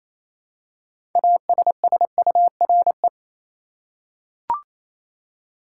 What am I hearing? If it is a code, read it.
Morse code, heard as a keyed tone, about 28 wpm, 700 Hz. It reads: ASSURE
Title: Six Letter Words - 28wpm